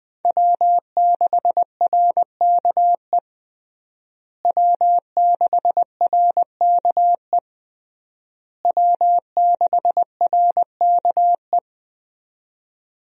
Morse code, heard as a keyed tone, about 20 wpm, 700 Hz. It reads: W6RKE W6RKE W6RKE
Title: USA Call Signs - 5 Characters - Repeated 3x - 20wpm